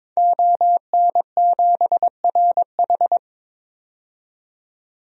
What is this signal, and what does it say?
Morse code, heard as a keyed tone, about 22 wpm, 700 Hz. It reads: ON7RH
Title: DXCC-100 Call Signs - 5 Characters - 22wpm